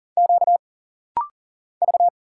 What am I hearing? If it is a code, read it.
Morse code, heard as a keyed tone, about 40 wpm, 700 Hz. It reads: Q V
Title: Single Letters - Rapid-Fire 40wpm